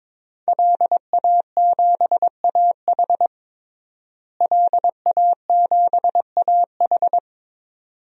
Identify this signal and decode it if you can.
Morse code, heard as a keyed tone, about 22 wpm, 700 Hz. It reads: LA7AH LA7AH
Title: DXCC-100 Call Signs - 5 Characters - Repeated 2x - 22wpm